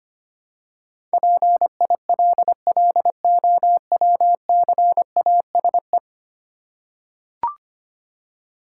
Morse code, heard as a keyed tone, about 25 wpm, 690 Hz. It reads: PILLOWCASE
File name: practice / 25wpm / Easy Words with Hints - 25wpm